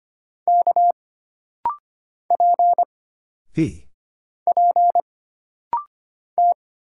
Morse code, heard as a keyed tone, about 25 wpm, 700 Hz. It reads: K P P T
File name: Single Character (Letter:Number:Punctuation:Prosign) - Rapid-Fire - 25wpm